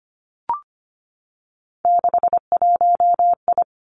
Morse code, heard as a keyed tone, about 25 wpm, 700 Hz. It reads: 61I
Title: Three Challenging Letter-Number Combinations 25wpm